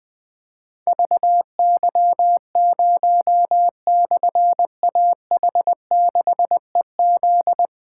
Morse code, heard as a keyed tone, about 20 wpm, 700 Hz. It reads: VY0/AH6EZ
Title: Difficult Call Signs 20wpm